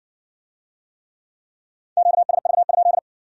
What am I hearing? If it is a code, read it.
Morse code, heard as a keyed tone, about 57 wpm, 700 Hz. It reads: QSV?